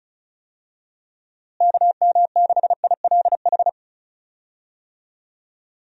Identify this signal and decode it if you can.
Morse code, heard as a keyed tone, about 34 wpm, 700 Hz. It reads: KM6ILH
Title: USA Call Signs - 6 Characters - Repeated 2x - 35wpm